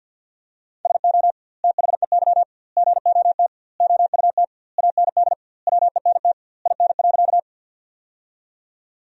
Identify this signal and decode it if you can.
Morse code, heard as a keyed tone, about 47 wpm, 700 Hz. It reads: SO THEY GOT OUT AND WENT IN.
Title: 2x Word-Spacing - Sentences from Top 300 Words - 50wpm